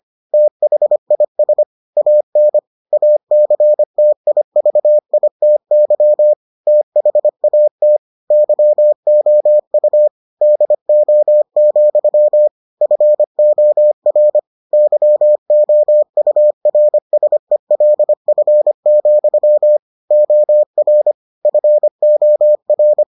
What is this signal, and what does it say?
Morse code, heard as a keyed tone, about 25 wpm, 600 Hz. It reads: THIS AN ACTIVITY THAT YOU DO, FOR YOURSELF, OR FOR